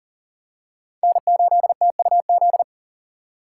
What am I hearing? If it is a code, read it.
Morse code, heard as a keyed tone, about 40 wpm, 700 Hz. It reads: N8TUZ
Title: USA Call Signs - 2x Character Spacing - 40wpm